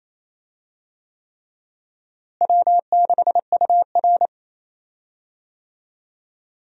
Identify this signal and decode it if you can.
Morse code, heard as a keyed tone, about 28 wpm, 700 Hz. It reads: W6UR
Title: USA Call Signs - 4 Characters - Repeated 3x - 28wpm